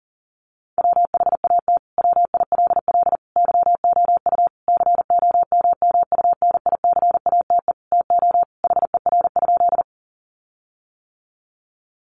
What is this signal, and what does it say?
Morse code, heard as a keyed tone, about 37 wpm, 700 Hz. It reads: WHAT WILL YOU COMMUNICATE TO HER?
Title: Sentences from Top 1300 Words 40wpm